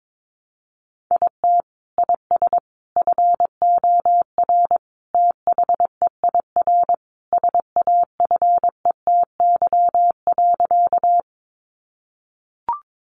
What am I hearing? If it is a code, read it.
Morse code, heard as a keyed tone, about 22 wpm, 700 Hz. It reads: IT IS FOR THEIR SAFETY.
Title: Sentences from Top 700 Words 22wpm